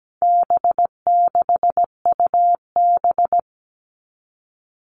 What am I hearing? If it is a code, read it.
Morse code, heard as a keyed tone, about 17 wpm, 700 Hz. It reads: B6UB
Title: All Cut Numbers - 4-Digit 17wpm